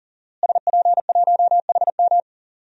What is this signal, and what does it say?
Morse code, heard as a keyed tone, about 40 wpm, 700 Hz. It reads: SP1HM
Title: DXCC-100 Call Signs - 5 Characters - Repeated 2x - 40wpm